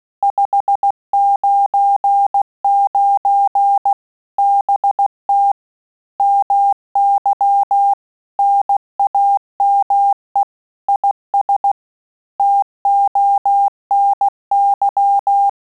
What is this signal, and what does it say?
Morse code, heard as a keyed tone, about 16 wpm, 800 Hz. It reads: 599 BT MY NAME IS TONY
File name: QSO CW 75 CPM